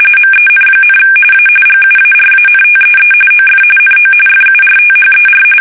RTTY, about 45.45 baud. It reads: MEN, A GUARANTEE OF NO FURTHER
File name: fsk850 / usnavy2